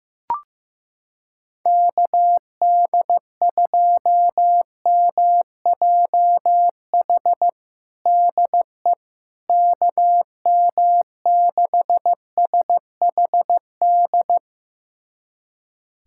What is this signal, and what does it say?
Morse code, heard as a keyed tone, about 15 wpm, 700 Hz. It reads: KD2MJH DE KM6SHD
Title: USA Callsign DE Callsign 15wpm